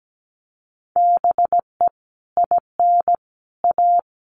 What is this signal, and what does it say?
Morse code, heard as a keyed tone, about 17 wpm, 700 Hz. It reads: BE IN A
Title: Sets of 5 Words Encore2 17wpm